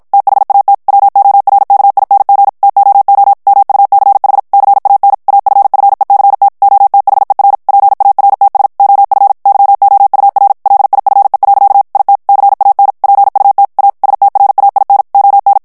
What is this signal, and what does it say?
Morse code, heard as a keyed tone, about 56 wpm, 800 Hz. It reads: THAT WORKING TOO MUCH CAN AFFECT OTHER PARTS OF YOUR LIFE. IT CAN PUT A STRAIN ON
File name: morse mp3 / ANGLAIS texte clair 1 vitesse 50 mots minute